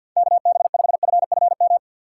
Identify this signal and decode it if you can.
Morse code, heard as a keyed tone, about 53 wpm, 700 Hz. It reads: KB5FFM
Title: USA Call Signs - 6 Characters - 3x Character Spacing - 50wpm